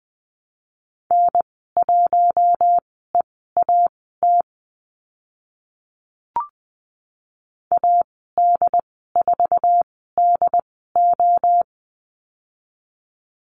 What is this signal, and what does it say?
Morse code, heard as a keyed tone, about 20 wpm, spaced as for 15 wpm, 700 Hz. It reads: N1EAT AD4DO
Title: 069 Lesson 12g - Letter d - Call Signs 20wpm